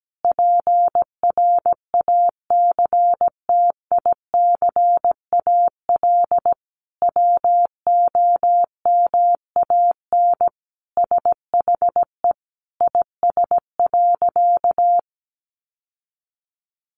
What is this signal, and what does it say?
Morse code, heard as a keyed tone, about 17 wpm, 700 Hz. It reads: PRACTICAL WOMAN SHE IS.
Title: Sentences from Top 1400 Words 17wpm